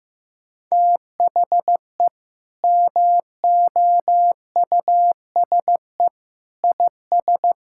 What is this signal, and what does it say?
Morse code, heard as a keyed tone, about 15 wpm, 700 Hz. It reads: THE MOUSE IS
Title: Sentences from Top 1000 Words - Review Part 5 - 15wpm